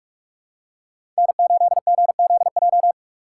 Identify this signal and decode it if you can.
Morse code, heard as a keyed tone, about 45 wpm, 700 Hz. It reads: N8GZJ
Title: USA Call Signs - 5 Characters - 6x Character Spacing - 45wpm